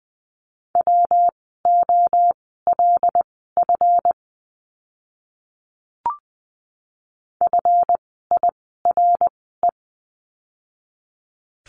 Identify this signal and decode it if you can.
Morse code, heard as a keyed tone, about 20 wpm, spaced as for 15 wpm, 700 Hz. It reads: WOLF FIRE
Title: 139 Lesson 23f - Letter f - Words 20wpm